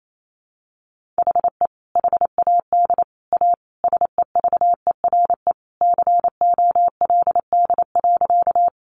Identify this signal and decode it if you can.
Morse code, heard as a keyed tone, about 28 wpm, 700 Hz. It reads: HE HAD A SEVERE COLD.
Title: Sentences from Top 1400 Words 28wpm